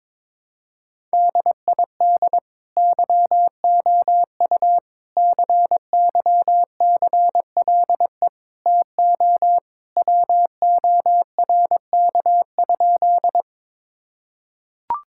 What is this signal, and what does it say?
Morse code, heard as a keyed tone, about 21 wpm, 700 Hz. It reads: DID YOU CYCLE TO WORK?
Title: Sentences from Top 1000 Words - Review Part 6 - 22wpm